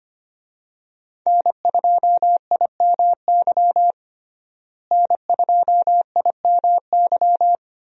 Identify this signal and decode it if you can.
Morse code, heard as a keyed tone, about 25 wpm, 690 Hz. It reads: N2IMY N2IMY
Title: USA Call Signs - 5 Characters - Repeated 2x - 25wpm